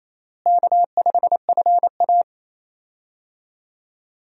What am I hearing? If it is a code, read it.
Morse code, heard as a keyed tone, about 28 wpm, 700 Hz. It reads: K5FA
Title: USA Call Signs - 4 Characters - Repeated 2x - 28wpm